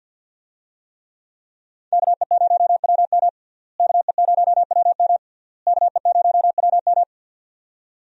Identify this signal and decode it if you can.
Morse code, heard as a keyed tone, about 52 wpm, 690 Hz. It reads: KE0WM KE0WM KE0WM
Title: USA Call Signs - 5 Characters - Repeated 3x - 50wpm